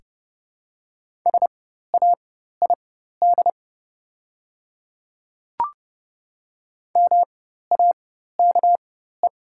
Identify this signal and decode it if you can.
Morse code, heard as a keyed tone, about 30 wpm, spaced as for 14 wpm, 700 Hz. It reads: SAID MAKE
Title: 4x Character Spacing - Top 100 Words - 30wpm